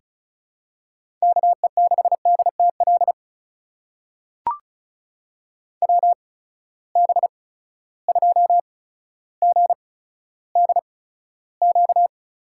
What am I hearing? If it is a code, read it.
Morse code, heard as a keyed tone, about 36 wpm, spaced as for 15 wpm, 700 Hz. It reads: KE6DTL WB2GDQ
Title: USA Call Signs - 6 Characters - 8x Character Spacing - 35wpm